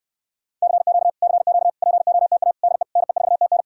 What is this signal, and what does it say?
Morse code, heard as a keyed tone, about 69 wpm, 700 Hz. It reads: CQ CQ POTA DE NE4TN